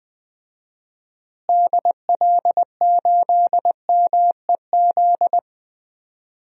Morse code, heard as a keyed tone, about 20 wpm, 700 Hz. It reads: DL8MEZ
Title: DXCC-100 Call Signs - 6 Characters - Repeated 2x - 20wpm